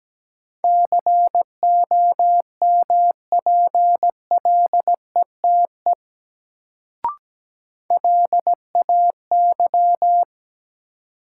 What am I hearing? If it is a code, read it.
Morse code, heard as a keyed tone, about 17 wpm, 700 Hz. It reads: COMPLETE LAY
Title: Sentences from Top 400 Words - New Word Drill - 17wpm